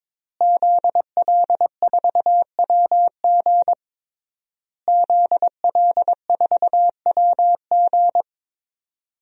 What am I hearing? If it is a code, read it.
Morse code, heard as a keyed tone, about 21 wpm, 700 Hz. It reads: ZL4WG ZL4WG
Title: DXCC-100 Call Signs - 5 Characters - Repeated 3x - 22wpm